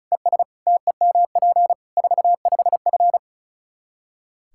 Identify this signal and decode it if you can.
Morse code, heard as a keyed tone, about 36 wpm, 700 Hz. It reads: ES TEMP 45F
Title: Practice QSO - Exchange 2 - Info - 35wpm